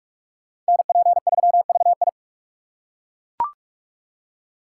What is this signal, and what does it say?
Morse code, heard as a keyed tone, about 41 wpm, 700 Hz. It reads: NP3VI